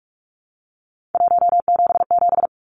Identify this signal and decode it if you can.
Morse code, heard as a keyed tone, about 44 wpm, 700 Hz. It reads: 177
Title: Numbers - 3-Digit 45wpm